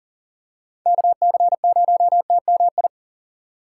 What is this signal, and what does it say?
Morse code, heard as a keyed tone, about 40 wpm, 700 Hz. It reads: KC0TMI
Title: USA Call Signs - 6 Characters - 3x Character Spacing - 40wpm